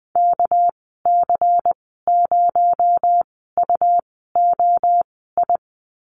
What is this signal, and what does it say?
Morse code, heard as a keyed tone, about 20 wpm, spaced as for 14 wpm, 700 Hz. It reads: KC0UOI
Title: 245 Lesson 39g - Letter z - Call Signs 20wpm